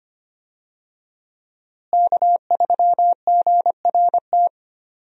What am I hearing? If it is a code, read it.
Morse code, heard as a keyed tone, about 25 wpm, 700 Hz. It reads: K3GRT
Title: USA Call Signs - 5 Characters - Repeated 3x - 25wpm